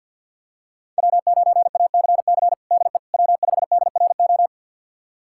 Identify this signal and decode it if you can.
Morse code, heard as a keyed tone, about 51 wpm, 700 Hz. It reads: W9ACC DE W5DRO
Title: USA Callsign DE Callsign - 5 Characters - 50wpm